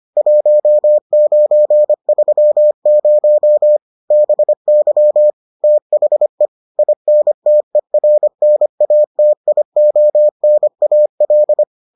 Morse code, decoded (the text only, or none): 1930 BY THE INTERNATIONAL